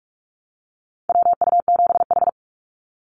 Morse code, heard as a keyed tone, about 46 wpm, 700 Hz. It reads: WU7H
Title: NA SOTA Activator Callsigns 45wpm